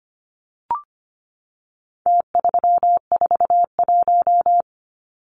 Morse code, heard as a keyed tone, about 25 wpm, 700 Hz. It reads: T341
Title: Numbers - Contest Style - 4-Digit 25wpm